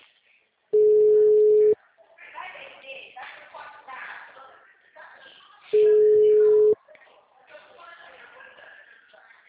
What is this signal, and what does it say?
Ringback tone (Europe, 425 Hz), with speech.